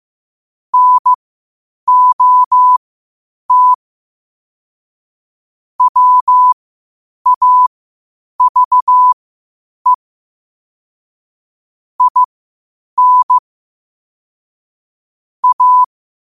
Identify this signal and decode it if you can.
Morse code, heard as a keyed tone, about 15 wpm, spaced as for 8 wpm, 1000 Hz. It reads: NOT WAVE IN A
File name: qotdfiles / 8wpm